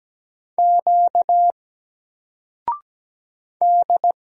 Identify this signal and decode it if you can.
Morse code, heard as a keyed tone, about 17 wpm, 700 Hz. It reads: Q D
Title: Single Letter-Number 17wpm